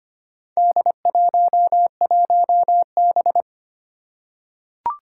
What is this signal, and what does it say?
Morse code, heard as a keyed tone, about 25 wpm, 700 Hz. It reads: D11B
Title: Four Challenging Letter-Number Combinations 25wpm